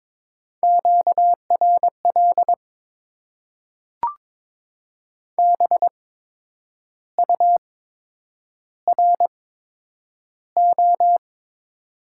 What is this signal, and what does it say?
Morse code, heard as a keyed tone, about 22 wpm, spaced as for 8 wpm, 700 Hz. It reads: QRL BURO
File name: Top 90 QSO Elements - 8x Character Spacing - 22wpm